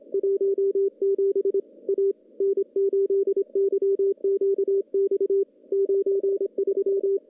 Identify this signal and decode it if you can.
Morse code, heard as a keyed tone, about 25 wpm, 390 Hz. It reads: 17 A N8YQX 93
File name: contests / CW-SS-k5tr- -0600